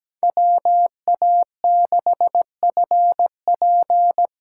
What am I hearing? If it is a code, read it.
Morse code, heard as a keyed tone, about 17 wpm, 700 Hz. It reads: WA6FP
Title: USA Call Signs - 6 Characters - 8x Character Spacing - 17wpm